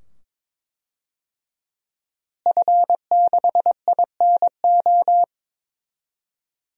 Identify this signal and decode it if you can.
Morse code, heard as a keyed tone, about 22 wpm, 700 Hz. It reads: F6INO
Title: DXCC-100 Call Signs - 5 Characters - Repeated 2x - 22wpm